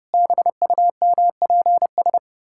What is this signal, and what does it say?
Morse code, heard as a keyed tone, about 30 wpm, 700 Hz. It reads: BUMPS
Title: Pangram Challenge v2 - 30wpm